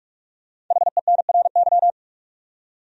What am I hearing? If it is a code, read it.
Morse code, heard as a keyed tone, about 46 wpm, 700 Hz. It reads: HENRY
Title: USA Common Names - 45wpm